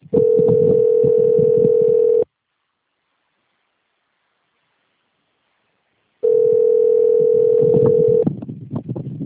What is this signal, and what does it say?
Ringback tone (North America), 440 + 480 Hz.
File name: rg-1600-unknown- - - .19246